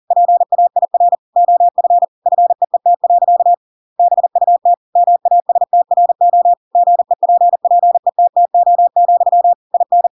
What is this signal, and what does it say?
Morse code, heard as a keyed tone, about 40 wpm, 700 Hz. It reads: PAIR OF FEET. BUT MASTRO GEPPETTO, IN